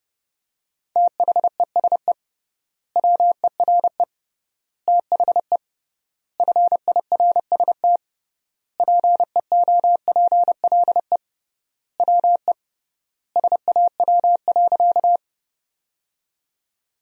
Morse code, heard as a keyed tone, about 31 wpm, 700 Hz. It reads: THESE WERE THE FIRST PEOPLE WE SAW.
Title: 3x Word-Spacing - Sentences from Top 300 Words - 30wpm